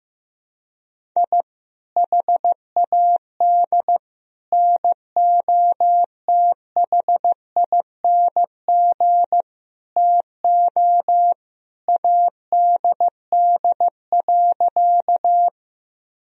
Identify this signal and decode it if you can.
Morse code, heard as a keyed tone, about 15 wpm, 700 Hz. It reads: I HAD NOTHING TO ADD.
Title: Sentences from Top 1000 Words - Review Part 1 - 15wpm